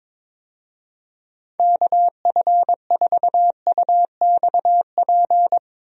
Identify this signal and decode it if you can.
Morse code, heard as a keyed tone, about 22 wpm, 700 Hz. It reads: KF4UXP
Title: USA Call Signs - 6 Characters - Repeated 3x - 22wpm